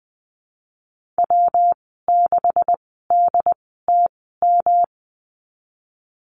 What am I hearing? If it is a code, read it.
Morse code, heard as a keyed tone, about 20 wpm, spaced as for 15 wpm, 700 Hz. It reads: W6DTM
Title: 173 Lesson 28g - Number 9 - Call Signs 20wpm